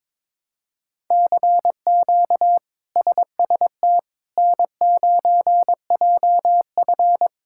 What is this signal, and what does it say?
Morse code, heard as a keyed tone, about 22 wpm, 700 Hz. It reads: CQ SST N9JF
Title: K1USN SST QSO Practice - 22wpm